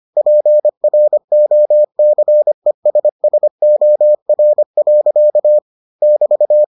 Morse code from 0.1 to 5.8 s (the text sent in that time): PROCESSOR.